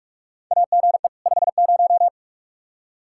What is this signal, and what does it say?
Morse code, heard as a keyed tone, about 42 wpm, 700 Hz. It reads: AGE 50
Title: Practice QSO - Exchange 3 - More Info - 45wpm